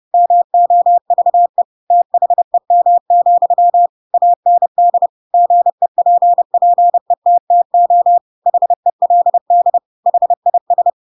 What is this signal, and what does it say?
Morse code, heard as a keyed tone, about 30 wpm, 700 Hz. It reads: MOVE THEM, AND GEPPETTO HELD HIS
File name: Pinocchio / Pinocchio ch3 30 30 ik1hge